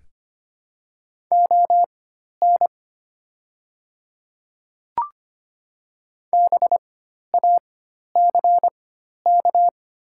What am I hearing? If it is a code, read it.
Morse code, heard as a keyed tone, about 25 wpm, spaced as for 12 wpm, 690 Hz. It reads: ON BACK
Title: 4x Character Spacing - Top 300 Words - 25wpm